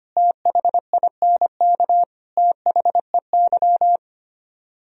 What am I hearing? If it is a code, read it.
Morse code, heard as a keyed tone, about 26 wpm, 700 Hz. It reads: THINK THEY
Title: Speed-Racing - Sets of 3 Words 25wpm